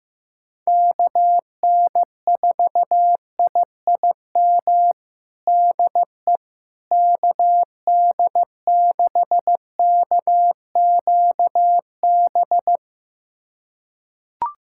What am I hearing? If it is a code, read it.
Morse code, heard as a keyed tone, about 15 wpm, 700 Hz. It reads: KN4IIM DE KD6KQB
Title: USA Callsign DE Callsign 15wpm